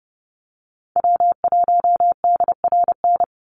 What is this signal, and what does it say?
Morse code, heard as a keyed tone, about 30 wpm, 700 Hz. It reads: W1DRN
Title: USA Call Signs 30wpm